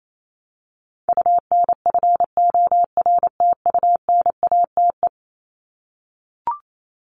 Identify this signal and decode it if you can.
Morse code, heard as a keyed tone, about 28 wpm, 700 Hz. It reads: UNFORTUNATE
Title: LY Suffix - Top 100 Words with Base Words 28wpm